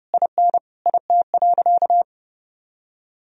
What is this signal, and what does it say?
Morse code, heard as a keyed tone, about 30 wpm, 700 Hz. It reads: IN IT.
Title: Sentences from Top 1000 Words - Review Part 4 - 30wpm